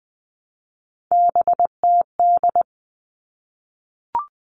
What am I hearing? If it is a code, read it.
Morse code, heard as a keyed tone, about 20 wpm, 700 Hz. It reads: BTD
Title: All Cut Numbers - 3-Digit 20wpm